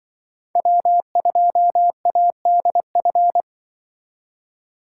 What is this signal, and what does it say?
Morse code, heard as a keyed tone, about 25 wpm, 700 Hz. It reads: W2ADF
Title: Speed Racing - USA Call Signs 20wpm